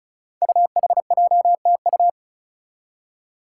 Morse code, heard as a keyed tone, about 36 wpm, 700 Hz. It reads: UHJTU
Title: Five Challenging Letter-Number Combinations 35wpm